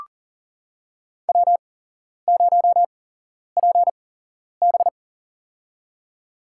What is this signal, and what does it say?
Morse code, heard as a keyed tone, about 40 wpm, spaced as for 11 wpm, 700 Hz. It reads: W0PB
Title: USA Call Signs - 4 Characters - 8x Character Spacing - 40wpm